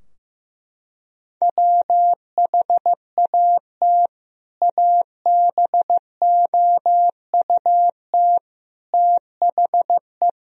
Morse code, heard as a keyed tone, about 15 wpm, 700 Hz. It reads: WHAT ABOUT THE
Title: Sentences from Top 1000 Words - Review Part 2 - 15wpm